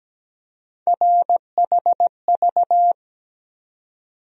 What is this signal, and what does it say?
Morse code, heard as a keyed tone, about 17 wpm, 700 Hz. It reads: RHV
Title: USA Call Sign Postfixes - 3 Letters 17wpm